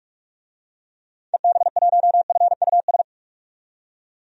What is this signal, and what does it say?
Morse code, heard as a keyed tone, about 46 wpm, 700 Hz. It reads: EB1FUS
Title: DXCC-100 Call Signs - 6 Characters - 45wpm